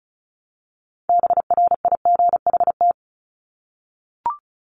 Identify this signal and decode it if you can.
Morse code, heard as a keyed tone, about 34 wpm, 700 Hz. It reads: BRIGHT
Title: Top 751-1000 Words 35wpm